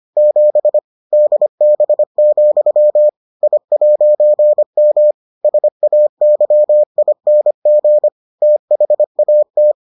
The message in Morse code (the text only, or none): 7 DB, I'M SAYING THAT